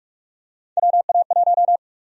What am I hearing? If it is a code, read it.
Morse code, heard as a keyed tone, about 42 wpm, 700 Hz. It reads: WA1
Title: USA Call Sign Prefixes - Repeated 2x - 45wpm